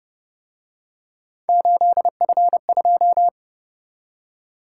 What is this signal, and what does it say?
Morse code, heard as a keyed tone, about 30 wpm, 700 Hz. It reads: 8F2
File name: Speed-Racing - Three Letter-Number Combinations - 25wpm